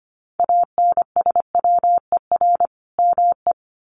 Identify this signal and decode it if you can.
Morse code, heard as a keyed tone, about 25 wpm, 700 Hz. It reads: ANSWER ME
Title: Sentences from Top 400 Words 25wpm